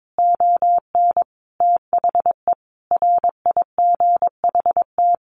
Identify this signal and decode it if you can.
Morse code, heard as a keyed tone, about 22 wpm, 700 Hz. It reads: ON THE RIGHT
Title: Sentences from Top 1400 Words 22wpm